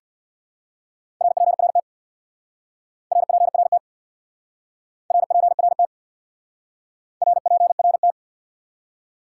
Morse code, heard as a keyed tone, about 65 wpm, 700 Hz. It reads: RPRT RPRT RPRT RPRT